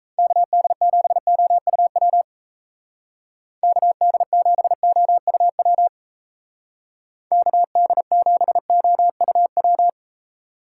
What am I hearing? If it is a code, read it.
Morse code, heard as a keyed tone, about 34 wpm, 700 Hz. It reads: KD7OUW KD7OUW KD7OUW
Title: Speed Racing - USA 2x3 Call Signs 28wpm